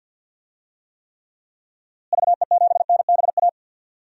Morse code, heard as a keyed tone, about 53 wpm, 700 Hz. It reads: VE7NBA
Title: DXCC-100 Call Signs - 6 Characters - Repeated 2x - 50wpm